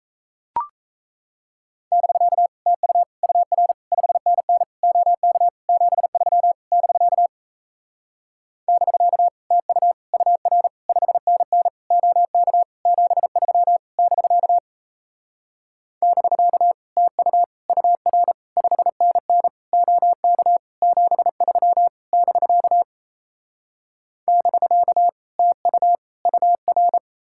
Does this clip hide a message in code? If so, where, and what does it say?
Yes: Morse code from 1.9 to 27.0 s, heard as a keyed tone, about 38 wpm, 700 Hz. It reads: <BK> TU UR 5NN OK 73 <BK> <BK> TU UR 5NN OK 73 <BK> <BK> TU UR 5NN OK 73 <BK> <BK> TU UR